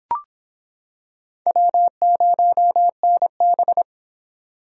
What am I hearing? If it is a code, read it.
Morse code, heard as a keyed tone, about 25 wpm, 700 Hz. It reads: W0NB